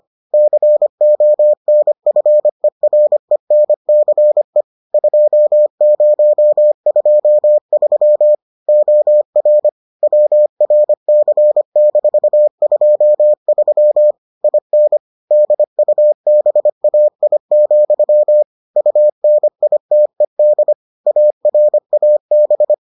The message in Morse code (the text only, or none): CONFERENCE 2023 OR WRC-23 IN DUBAI, UNITED ARAB